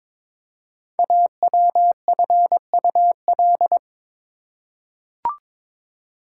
Morse code, heard as a keyed tone, about 23 wpm, 700 Hz. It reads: AWFUL
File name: 5 Letter Words - First 3 Letters Given - 22wpm